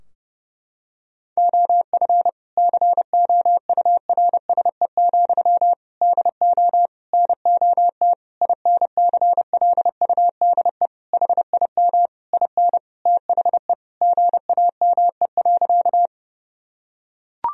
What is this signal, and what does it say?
Morse code, heard as a keyed tone, about 30 wpm, 700 Hz. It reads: OF COURSE, DO NOT INCLUDE HIM IN THE GAME.
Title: Sentences from Top 1000 Words - Review Part 3 - 30wpm